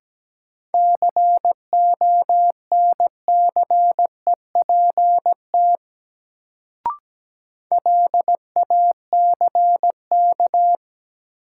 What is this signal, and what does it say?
Morse code, heard as a keyed tone, about 17 wpm, 700 Hz. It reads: CONCEPT LACK
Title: Sentences from Top 1100 Words - New Word Drill Alt - 17wpm